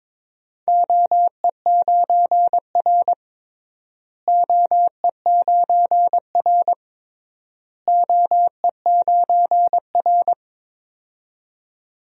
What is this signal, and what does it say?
Morse code, heard as a keyed tone, about 22 wpm, 700 Hz. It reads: OE9R OE9R OE9R
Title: DXCC-100 Call Signs - 4 Characters - Repeated 3x - 22wpm